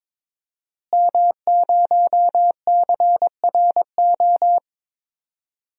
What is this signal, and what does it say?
Morse code, heard as a keyed tone, about 22 wpm, 700 Hz. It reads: M0CRO
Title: DXCC-100 Call Signs - 5 Characters - Repeated 2x - 22wpm